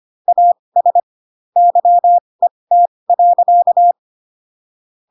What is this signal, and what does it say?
Morse code, heard as a keyed tone, about 25 wpm, spaced as for 20 wpm, 700 Hz. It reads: AS YET.
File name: Pinocchio / Pinocchio ch5 15 25 ik1hge